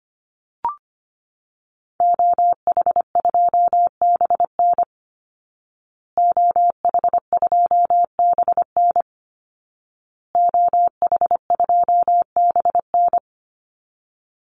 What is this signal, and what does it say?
Morse code, heard as a keyed tone, about 25 wpm, 700 Hz. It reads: OH2BN OH2BN OH2BN
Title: Call Signs Repeated 2x - 25wpm